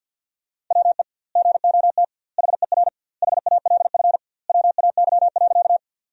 Morse code, heard as a keyed tone, about 50 wpm, 700 Hz. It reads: WE GOT HER HALF WAY.
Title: 2x Word-Spacing - Sentences from Top 400 Words - 50wpm